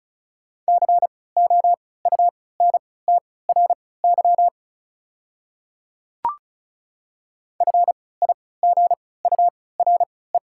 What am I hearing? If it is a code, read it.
Morse code, heard as a keyed tone, about 34 wpm, spaced as for 20 wpm, 700 Hz. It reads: COUNTRY FIGURE
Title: 3x Character Spacing - Top 500 Words - 35wpm